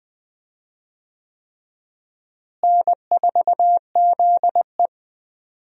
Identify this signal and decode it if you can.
Morse code, heard as a keyed tone, about 20 wpm, 700 Hz. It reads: N4ZE
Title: USA Call Signs - 4 Characters - Repeated 3x - 20wpm